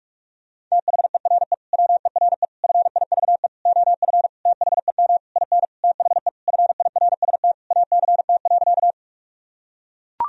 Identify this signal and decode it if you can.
Morse code, heard as a keyed tone, about 41 wpm, 700 Hz. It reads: THERE WERE FIVE OF THEM IN THE FIRST ACT.
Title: Sentences from Top 1000 Words - Review Part 2 - 45wpm